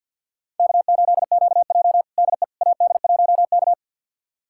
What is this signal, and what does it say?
Morse code, heard as a keyed tone, about 52 wpm, 700 Hz. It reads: K8QJ DE AD1K
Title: USA Callsign DE Callsign - 4 Characters - 50wpm